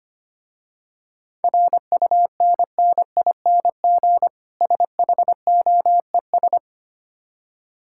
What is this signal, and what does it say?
Morse code, heard as a keyed tone, about 25 wpm, 700 Hz. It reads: RUNNING SHOES
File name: practice / 25wpm / Difficult Words with Hints - 25wpm